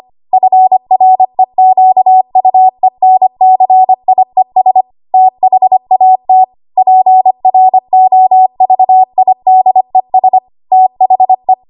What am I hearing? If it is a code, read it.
Morse code, heard as a keyed tone, about 25 wpm, 750 Hz. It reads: FREQUENCIES THAT PROVIDES THE
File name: Archive / 25 WPM / 25WPM